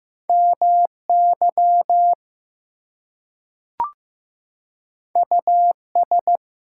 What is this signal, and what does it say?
Morse code, heard as a keyed tone, about 15 wpm, 700 Hz. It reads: MY US
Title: Two Letter Words - 15wpm